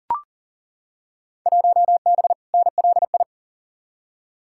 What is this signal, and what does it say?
Morse code, heard as a keyed tone, about 40 wpm, 700 Hz. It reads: 1B NLI
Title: Field Day Exchanges - 40wpm